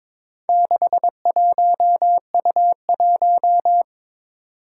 Morse code, heard as a keyed tone, about 23 wpm, 700 Hz. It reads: 61U1
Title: Four Challenging Letter-Number Combinations 22wpm